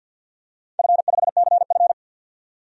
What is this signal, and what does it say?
Morse code, heard as a keyed tone, about 52 wpm, 700 Hz. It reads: F5CF
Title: DXCC-100 Call Signs - 4 Characters - 50wpm